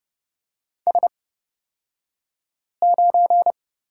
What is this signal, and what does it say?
Morse code, heard as a keyed tone, about 29 wpm, spaced as for 5 wpm, 700 Hz. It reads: S9
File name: Lesson 36d - BT - Two Characters 30wpm